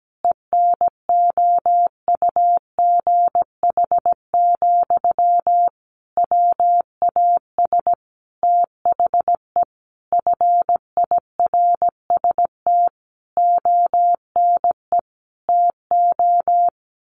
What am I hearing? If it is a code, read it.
Morse code, heard as a keyed tone, about 17 wpm, 700 Hz. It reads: ENOUGH, WAS THE FIRST ONE TO
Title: Sentences from Top 1300 Words 17wpm